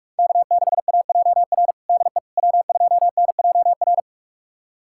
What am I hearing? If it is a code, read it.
Morse code, heard as a keyed tone, about 46 wpm, 700 Hz. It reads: K6AJR DE W2NJR
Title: USA Callsign DE Callsign - 5 Characters - 45wpm